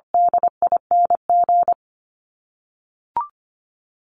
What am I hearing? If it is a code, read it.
Morse code, heard as a keyed tone, about 25 wpm, 700 Hz. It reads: DING
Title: ING Suffix - Top 100 Words with Base Words 25wpm